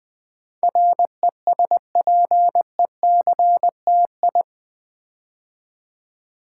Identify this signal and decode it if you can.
Morse code, heard as a keyed tone, about 20 wpm, 700 Hz. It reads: RESPECTI
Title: Build A Word - 4 Letters At A Time - 12-Letter Words - 20wpm